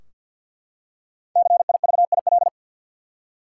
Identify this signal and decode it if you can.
Morse code, heard as a keyed tone, about 45 wpm, 700 Hz. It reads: CIVIL
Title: Five Letter Words - 50wpm